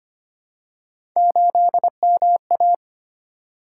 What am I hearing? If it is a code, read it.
Morse code, heard as a keyed tone, about 25 wpm, 700 Hz. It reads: 8MA
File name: Three Letter-Number Combinations - 25wpm